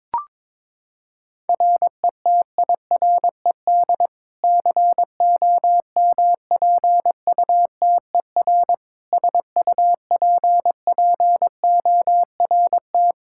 Morse code, heard as a keyed tone, about 22 wpm, 700 Hz. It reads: RETIRED COMPUTER SUPPORT
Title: Practice QSO - Exchange 3 - More Info - 22wpm